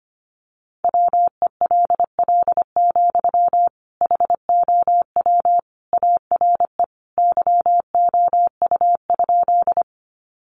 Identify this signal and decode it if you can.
Morse code, heard as a keyed tone, about 25 wpm, 700 Hz. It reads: WELL, HOW ARE YOU?
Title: Sentences from Top 200 Words 25wpm